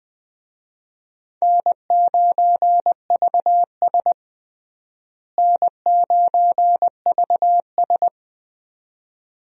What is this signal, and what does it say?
Morse code, heard as a keyed tone, about 20 wpm, 700 Hz. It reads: N9VS N9VS
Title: USA Call Signs - 4 Characters - Repeated 2x - 20wpm